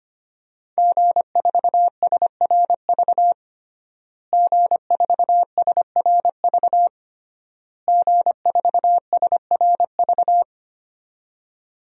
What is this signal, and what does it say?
Morse code, heard as a keyed tone, about 25 wpm, 700 Hz. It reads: G4SRV G4SRV G4SRV
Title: DXCC-100 Call Signs - 5 Characters - Repeated 3x - 25wpm